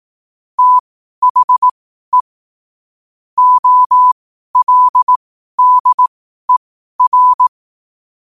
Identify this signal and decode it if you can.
Morse code, heard as a keyed tone, about 18 wpm, spaced as for 12 wpm, 1000 Hz. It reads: THE OLDER
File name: qotdfiles / 12wpm